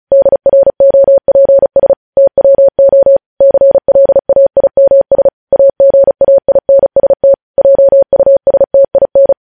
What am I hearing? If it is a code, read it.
Morse code, heard as a keyed tone, about 36 wpm, 550 Hz. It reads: DROPS TWO CLAIMS AGAINST JUSTIN
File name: podcast / News headlines-1x-35x35